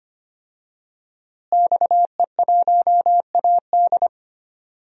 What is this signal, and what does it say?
Morse code, heard as a keyed tone, about 25 wpm, 700 Hz. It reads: XE1AD
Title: DXCC-100 Call Signs - 5 Characters - Repeated 2x - 25wpm